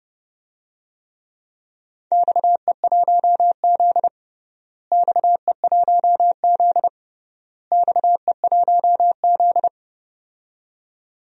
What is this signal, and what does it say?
Morse code, heard as a keyed tone, about 30 wpm, 700 Hz. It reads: XE1Z XE1Z XE1Z
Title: DXCC-100 Call Signs - 4 Characters - Repeated 3x - 30wpm